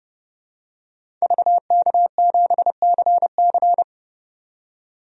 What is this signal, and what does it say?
Morse code, heard as a keyed tone, about 30 wpm, 700 Hz. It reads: VK7CC
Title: DXCC-100 Call Signs - 5 Characters - 30wpm